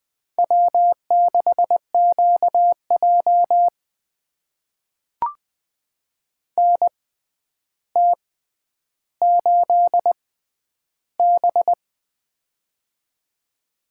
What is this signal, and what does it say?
Morse code, heard as a keyed tone, about 21 wpm, spaced as for 10 wpm, 700 Hz. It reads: W6QJ NT8B
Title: USA Call Signs - 4 Characters - 6x Character Spacing - 20wpm